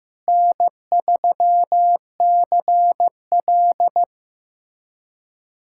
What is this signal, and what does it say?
Morse code, heard as a keyed tone, about 15 wpm, 700 Hz. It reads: N3CL
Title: USA Callsign DE Callsign - 4 Characters - 15wpm